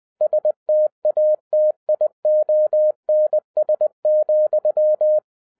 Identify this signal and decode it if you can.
Morse code, heard as a keyed tone, about 20 wpm, 610 Hz. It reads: STATIONS,